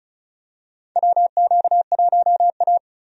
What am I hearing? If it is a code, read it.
Morse code, heard as a keyed tone, about 36 wpm, 700 Hz. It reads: WQ1A
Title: USA Call Signs - 4 Characters - Repeated 2x - 35wpm